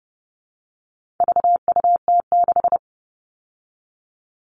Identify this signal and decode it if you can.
Morse code, heard as a keyed tone, about 30 wpm, 700 Hz. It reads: VUT6
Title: All Cut Numbers - 4-Digit 30wpm